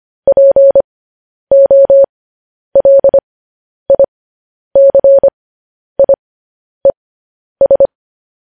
Morse code, heard as a keyed tone, about 25 wpm, spaced as for 10 wpm, 550 Hz. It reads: POLICIES
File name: podcast / News headlines-1x-25x10